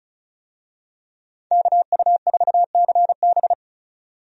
Morse code, heard as a keyed tone, about 36 wpm, 700 Hz. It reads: KU4CB
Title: USA Call Signs - 5 Characters - 35wpm